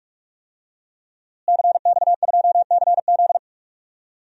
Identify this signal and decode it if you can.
Morse code, heard as a keyed tone, about 46 wpm, 700 Hz. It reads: CX2CZ
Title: DXCC-100 Call Signs - 5 Characters - Repeated 3x - 45wpm